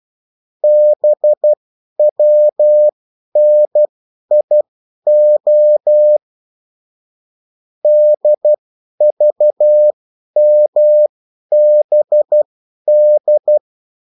Morse code, heard as a keyed tone, about 12 wpm, spaced as for 10 wpm, 600 Hz. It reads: BWNIO DVMBD
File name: morse mp3 / revision9-vitesse 10 mots